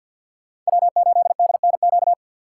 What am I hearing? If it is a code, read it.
Morse code, heard as a keyed tone, about 62 wpm, 700 Hz. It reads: W8DNQ
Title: Speed Racing - USA Call Signs 50wpm